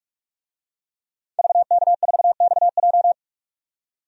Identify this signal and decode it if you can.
Morse code, heard as a keyed tone, about 47 wpm, 700 Hz. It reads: VK4XJ
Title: DXCC-100 Call Signs - 5 Characters - 45wpm